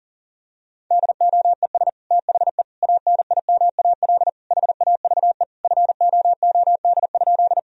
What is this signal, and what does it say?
Morse code, heard as a keyed tone, about 40 wpm, 700 Hz. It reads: DOES THE ANIMAL HAVE FOOD?
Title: Sentences from Top 1000 Words - Review Part 1 - 40wpm